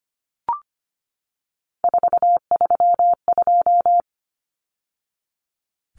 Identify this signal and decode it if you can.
Morse code, heard as a keyed tone, about 25 wpm, 700 Hz. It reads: 432
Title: Numbers - 3-Digit 25wpm